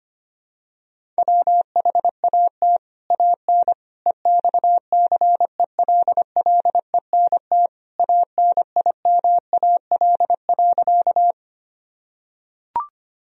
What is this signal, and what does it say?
Morse code, heard as a keyed tone, about 25 wpm, 700 Hz. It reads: WHAT AN EXCELLENT ANIMAL.
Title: Sentences from Top 1000 Words - Review Part 1 - 25wpm